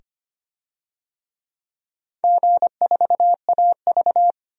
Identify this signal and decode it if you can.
Morse code, heard as a keyed tone, about 26 wpm, 700 Hz. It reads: G4AV
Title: DXCC-100 Call Signs - 4 Characters - 25wpm